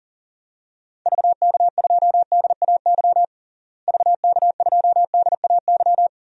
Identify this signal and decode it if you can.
Morse code, heard as a keyed tone, about 40 wpm, 700 Hz. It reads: VK2DAY VK2DAY
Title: DXCC-100 Call Signs - 6 Characters - Repeated 3x - 40wpm